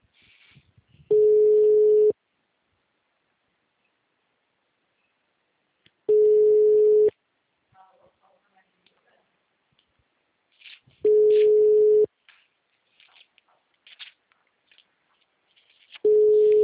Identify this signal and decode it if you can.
Ringback tone (Europe, 425 Hz).